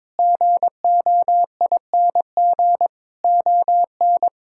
Morse code, heard as a keyed tone, about 22 wpm, 700 Hz. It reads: GOING ON
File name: practice / 22wpm / Sets of 7 Words - 22wpm